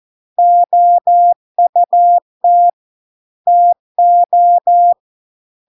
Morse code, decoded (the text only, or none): OUT TO